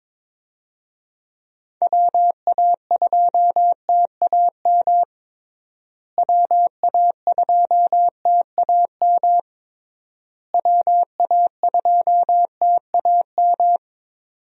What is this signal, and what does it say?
Morse code, heard as a keyed tone, about 22 wpm, 700 Hz. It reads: WA2TAM WA2TAM WA2TAM
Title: USA Call Signs - 6 Characters - Repeated 3x - 22wpm